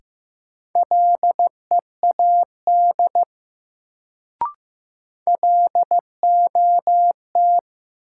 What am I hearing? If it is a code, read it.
Morse code, heard as a keyed tone, about 15 wpm, 700 Hz. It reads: LEAD LOT
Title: Sentences from Top 500 Words - New Word Drill - 15wpm